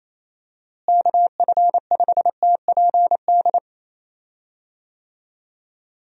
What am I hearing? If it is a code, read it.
Morse code, heard as a keyed tone, about 28 wpm, 700 Hz. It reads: KF5TPD
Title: USA Call Signs - 6 Characters - Repeated 2x - 28wpm